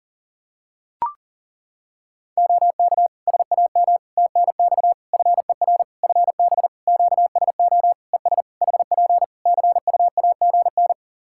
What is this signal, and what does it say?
Morse code, heard as a keyed tone, about 37 wpm, 700 Hz. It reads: OK SAM TNX FER FB QSO ES HP CUAGN
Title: Practice QSO - Exchange 4 - Ending - 40wpm